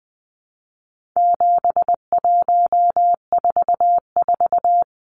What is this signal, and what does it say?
Morse code, heard as a keyed tone, about 20 wpm, 700 Hz. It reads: 7144
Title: Numbers - 4-Digit 20wpm